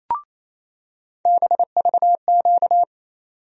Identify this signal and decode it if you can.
Morse code, heard as a keyed tone, about 28 wpm, 700 Hz. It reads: BVQ
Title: USA Call Sign Postfixes - 3 Letters 28wpm